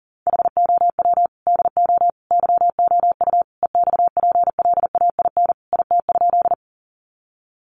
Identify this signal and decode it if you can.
Morse code, heard as a keyed tone, about 40 wpm, 700 Hz. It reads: HOW DO YOU EXPLAIN IT?
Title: Sentences from Top 1100 Words 40wpm